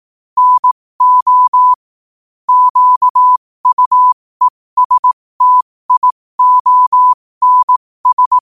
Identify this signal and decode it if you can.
Morse code, heard as a keyed tone, about 18 wpm, spaced as for 15 wpm, 1000 Hz. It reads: NO QUESTIONS
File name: qotdfiles / 15wpm